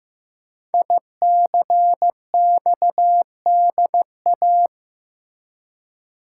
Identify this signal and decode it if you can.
Morse code, heard as a keyed tone, about 15 wpm, 700 Hz. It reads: ICXDA
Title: Code Groups - 5 Letter:Numbers - 15wpm